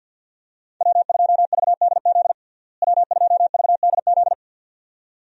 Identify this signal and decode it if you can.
Morse code, heard as a keyed tone, about 52 wpm, 700 Hz. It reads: W2VDZ W2VDZ
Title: USA Call Signs - 5 Characters - Repeated 2x - 50wpm